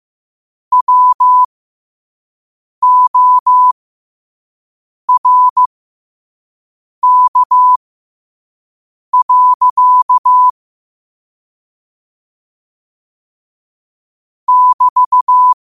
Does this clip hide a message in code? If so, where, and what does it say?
Yes: Morse code from 0.7 to 15.5 s, heard as a keyed tone, about 15 wpm, spaced as for 5 wpm, 1000 Hz. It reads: WORK. <BT>